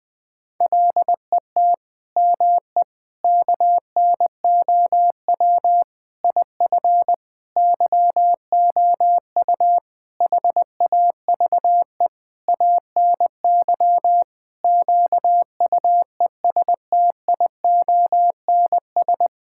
Morse code, heard as a keyed tone, about 20 wpm, 700 Hz. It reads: LET ME KNOW IF YOU HAVE ANY QUESTIONS
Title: Sets of 8 Words - 20wpm